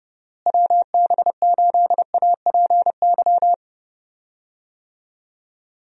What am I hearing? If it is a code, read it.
Morse code, heard as a keyed tone, about 31 wpm, 700 Hz. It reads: WB8APY
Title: USA Call Signs - 6 Characters - Repeated 2x - 30wpm